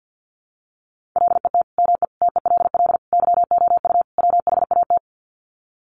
Morse code, heard as a keyed tone, about 50 wpm, 700 Hz. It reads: LET ME TELL YOU WHAT
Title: Sets of 5 Words Encore 50wpm